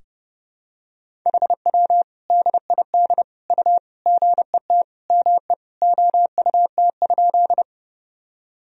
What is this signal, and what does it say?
Morse code, heard as a keyed tone, about 32 wpm, 700 Hz. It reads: HW DID U GET ME OUT?
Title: Sentences from Top 200 Words and Abbreviations 30wpm